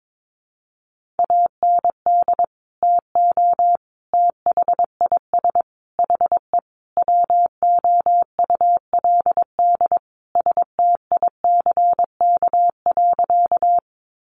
Morse code, heard as a keyed tone, about 22 wpm, 700 Hz. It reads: AND TO THIS HE WOULD STICK.
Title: Sentences from Top 1200 Words 22wpm